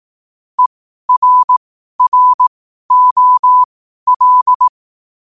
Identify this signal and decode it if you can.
Morse code, heard as a keyed tone, about 18 wpm, spaced as for 12 wpm, 1000 Hz. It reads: ERROL
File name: 12wpm